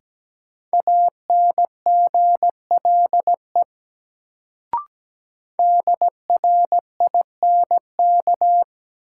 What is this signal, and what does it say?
Morse code, heard as a keyed tone, about 17 wpm, 700 Hz. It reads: ANGLE DRINK
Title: Sentences from Top 1500 Words - New Word Drill Alt - 17wpm